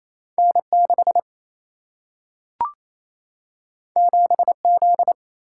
Morse code, heard as a keyed tone, about 28 wpm, 700 Hz. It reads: N 6 7 Z
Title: Single-Letter Single-Number 28wpm